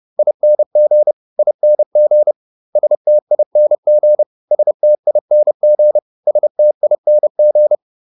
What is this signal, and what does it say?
Morse code, heard as a keyed tone, about 30 wpm, 600 Hz. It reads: ING ING STING STING STING